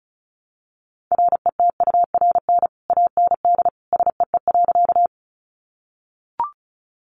Morse code, heard as a keyed tone, about 34 wpm, 700 Hz. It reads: RETURN AND SEE.
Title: Sentences from Top 800 Words 35wpm